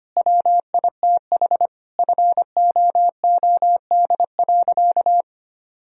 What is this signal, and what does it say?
Morse code, heard as a keyed tone, about 25 wpm, 700 Hz. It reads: WITH FOOD.
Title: Sentences from Top 1000 Words - Review Part 4 - 25wpm